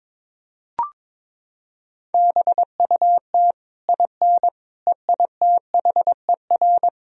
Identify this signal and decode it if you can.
Morse code, heard as a keyed tone, about 22 wpm, 700 Hz. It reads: BUT IN EITHER
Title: Sentences from Top 1000 Words - Review Part 4 - 22wpm